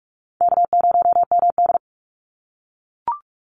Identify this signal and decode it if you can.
Morse code, heard as a keyed tone, about 44 wpm, 700 Hz. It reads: K0MD
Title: Easy Call Signs - 45wpm